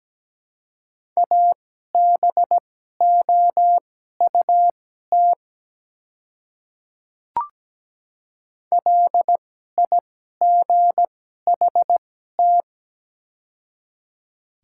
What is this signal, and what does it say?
Morse code, heard as a keyed tone, about 17 wpm, spaced as for 12 wpm, 700 Hz. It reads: ABOUT LIGHT
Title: 2x Character Spacing - Top 300 Words - 17wpm